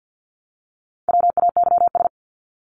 Morse code, heard as a keyed tone, about 50 wpm, 700 Hz. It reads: WAYS
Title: ES:S Suffix - Top 100 Words with Base Words 50wpm